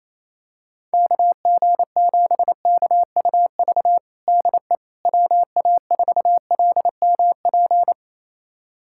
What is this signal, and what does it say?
Morse code, heard as a keyed tone, about 28 wpm, 700 Hz. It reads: KG7KUV DE WA4LMP
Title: USA Callsign DE Callsign - 6 Characters - 28wpm